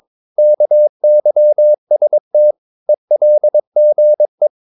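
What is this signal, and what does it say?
Morse code, heard as a keyed tone, about 22 wpm, 600 Hz. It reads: KYST ELGE
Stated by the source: Morse practice of short Danish words at 22 wpm, from CW med Gnister.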